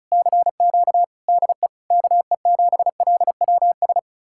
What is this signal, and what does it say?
Morse code, heard as a keyed tone, about 46 wpm, spaced as for 41 wpm, 700 Hz. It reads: CQ DE KE7LWS
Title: Speed-Racing Common QSO Phrases Encore - 35wpm